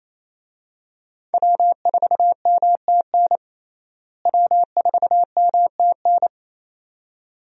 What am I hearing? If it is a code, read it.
Morse code, heard as a keyed tone, about 28 wpm, 700 Hz. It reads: W4MTN W4MTN
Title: USA Call Signs - 5 Characters - Repeated 2x - 28wpm